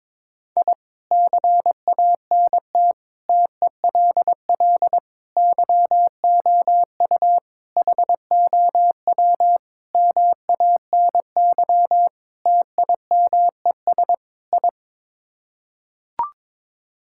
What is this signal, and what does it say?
Morse code, heard as a keyed tone, about 22 wpm, 700 Hz. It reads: I CANT TELL YOU HOW MANY TIMES I
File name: practice / 22wpm / Sets of 8 Words - 22wpm